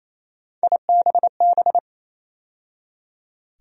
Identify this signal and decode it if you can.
Morse code, heard as a keyed tone, about 28 wpm, 700 Hz. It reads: IBB
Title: USA Call Sign Postfixes - 3 Letters 28wpm